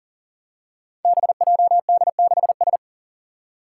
Morse code, heard as a keyed tone, about 42 wpm, 690 Hz. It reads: BJD6S
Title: Five Challenging Letter-Number Combinations 40wpm